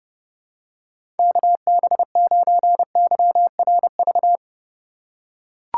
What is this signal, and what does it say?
Morse code, heard as a keyed tone, about 32 wpm, 700 Hz. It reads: KB9YRV
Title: USA Call Signs - 6 Characters - 2x Character Spacing - 30wpm